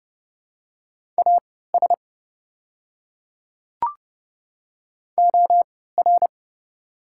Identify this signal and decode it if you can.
Morse code, heard as a keyed tone, about 30 wpm, spaced as for 17 wpm, 700 Hz. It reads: AS OR
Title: 3x Character Spacing - Top 100 Words - 30wpm